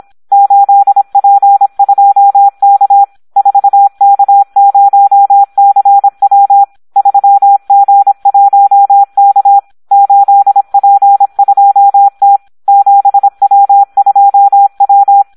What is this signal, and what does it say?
Morse code, heard as a keyed tone, about 26 wpm, 800 Hz. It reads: 8P2K 4K0CW 3G1K 8P2T 7W2W
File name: Audio practica CW / SD 1 26